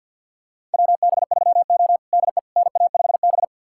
Morse code, heard as a keyed tone, about 53 wpm, 690 Hz. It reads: WB3O DE NA5B
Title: USA Callsign DE Callsign - 4 Characters - 50wpm